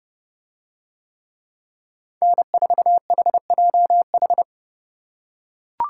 Morse code, heard as a keyed tone, about 31 wpm, 700 Hz. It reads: N4HJH